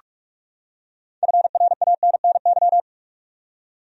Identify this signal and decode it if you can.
Morse code, heard as a keyed tone, about 46 wpm, 700 Hz. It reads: FRANNY
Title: USA Common Names - 45wpm